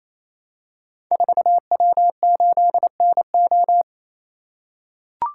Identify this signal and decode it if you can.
Morse code, heard as a keyed tone, about 29 wpm, 700 Hz. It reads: 4W8NO
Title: Code Groups - 5 Letter:Numbers - 28wpm